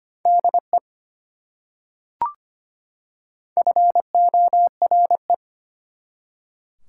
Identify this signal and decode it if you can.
Morse code, heard as a keyed tone, about 25 wpm, 700 Hz. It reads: DE FORE
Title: Prefixes and Suffixes 25wpm